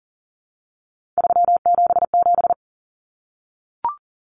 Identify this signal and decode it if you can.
Morse code, heard as a keyed tone, about 40 wpm, 700 Hz. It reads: 377
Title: Common Cut Numbers - 3-Digit 40wpm